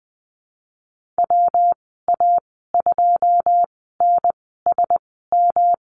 Morse code, heard as a keyed tone, about 20 wpm, spaced as for 15 wpm, 700 Hz. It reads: WA2NSM
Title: 122 Lesson 20g - Number 3 - Call Signs 20wpm